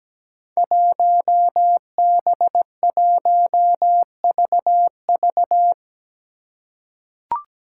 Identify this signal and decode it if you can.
Morse code, heard as a keyed tone, about 17 wpm, 700 Hz. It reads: 1B1VV
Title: Five Challenging Letter-Number Combinations 17wpm